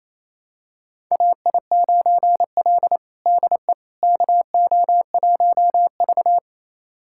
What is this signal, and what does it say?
Morse code, heard as a keyed tone, about 28 wpm, 700 Hz. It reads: AI9L DE KO1V
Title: USA Callsign DE Callsign - 4 Characters - 28wpm